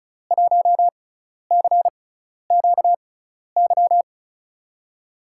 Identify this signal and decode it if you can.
Morse code, heard as a keyed tone, about 34 wpm, spaced as for 12 wpm, 700 Hz. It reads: 1CQY
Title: USA Call Signs - 6 Characters - 6x Character Spacing - 35wpm